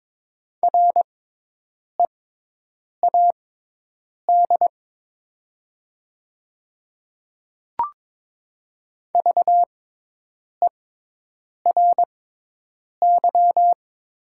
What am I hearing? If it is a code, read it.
Morse code, heard as a keyed tone, about 22 wpm, spaced as for 8 wpm, 700 Hz. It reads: READ VERY
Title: 6x Character Spacing - Top 200 Words - 22wpm